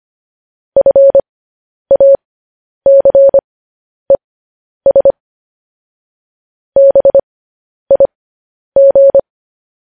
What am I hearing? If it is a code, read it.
Morse code, heard as a keyed tone, about 25 wpm, spaced as for 10 wpm, 550 Hz. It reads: FACES BIG